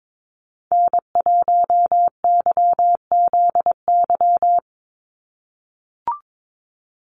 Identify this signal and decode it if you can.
Morse code, heard as a keyed tone, about 22 wpm, 700 Hz. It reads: N1YZY
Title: USA Call Signs 22wpm